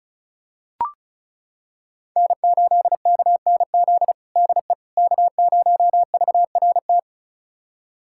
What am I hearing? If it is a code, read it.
Morse code, heard as a keyed tone, about 34 wpm, 700 Hz. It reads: N8KNZ DE K0VRT
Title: USA Callsign DE Callsign - 5 Characters - 35wpm